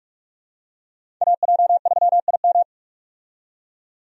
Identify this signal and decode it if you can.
Morse code, heard as a keyed tone, about 46 wpm, 700 Hz. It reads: AJ3IM
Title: USA Call Signs - 5 Characters - 45wpm